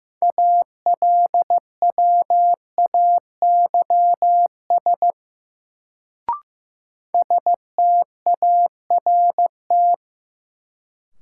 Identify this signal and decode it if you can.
Morse code, heard as a keyed tone, about 15 wpm, 700 Hz. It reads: ALWAYS START
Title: Sentences from Top 300 Words - New Word Drill - 15wpm